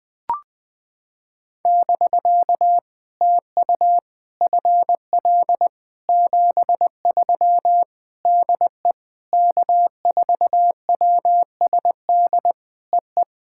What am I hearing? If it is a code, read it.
Morse code, heard as a keyed tone, about 20 wpm, 700 Hz. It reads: <BK> TU FL 73 DE K4WSD EE
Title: POTA QSO Practice - 20wpm